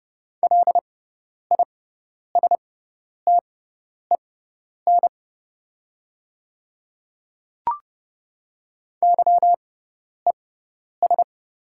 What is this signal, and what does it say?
Morse code, heard as a keyed tone, about 30 wpm, spaced as for 10 wpm, 700 Hz. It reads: LISTEN YES
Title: 6x Character Spacing - Top 500 Words - 30wpm